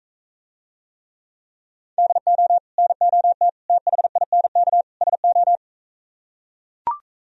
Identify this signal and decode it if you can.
Morse code, heard as a keyed tone, about 43 wpm, 700 Hz. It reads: DO NOT THINK SO